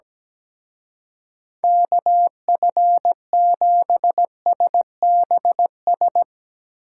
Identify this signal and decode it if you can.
Morse code, heard as a keyed tone, about 18 wpm, 700 Hz. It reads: KF7SBS
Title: Speed Racing - USA 2x3 Call Signs 17wpm